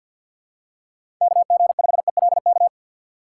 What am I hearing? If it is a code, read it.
Morse code, heard as a keyed tone, about 52 wpm, 700 Hz. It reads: KG5ELK
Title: USA Call Signs - 6 Characters - 3x Character Spacing - 50wpm